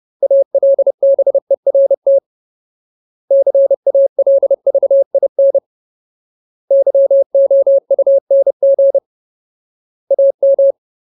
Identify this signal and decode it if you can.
Morse code, heard as a keyed tone, about 30 wpm, 550 Hz. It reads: ALBERT CALVIN YOUNG AM
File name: Ultimate Word List / 30WPM Ultimate